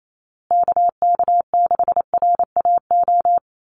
Morse code, heard as a keyed tone, about 28 wpm, 700 Hz. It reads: KK6RAO
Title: USA Call Signs Repeated 3x 28wpm